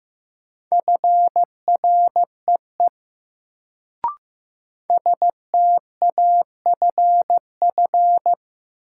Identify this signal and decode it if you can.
Morse code, heard as a keyed tone, about 15 wpm, 700 Hz. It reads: FREE STAFF
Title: Sets of 5 Words Encore - New Word Drill - 15wpm